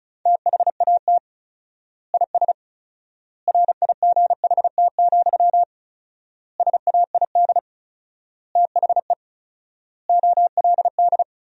Morse code, heard as a keyed tone, about 34 wpm, 700 Hz. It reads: THAT IS RIGHT, SAID THE OLD
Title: 4x Word-Spacing - Sentences from Top 200 Words - 35wpm